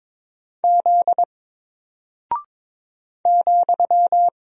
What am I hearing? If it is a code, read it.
Morse code, heard as a keyed tone, about 22 wpm, 700 Hz. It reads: Z ,
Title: Single Character (Letter:Number:Punctuation:Prosign) Alt 22wpm